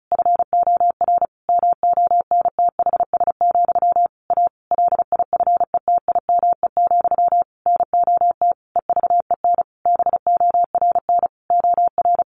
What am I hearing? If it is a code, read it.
Morse code, heard as a keyed tone, about 34 wpm, 700 Hz. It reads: FOR MONTHS, A LIFETIME, NOT EVEN BORN OR
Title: A Sound Of Thunder 35wpm